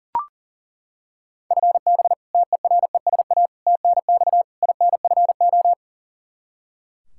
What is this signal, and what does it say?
Morse code, heard as a keyed tone, about 40 wpm, 700 Hz. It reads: FB TERESA TNX INFO
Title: Practice QSO - Exchange 3 - More Info - 40wpm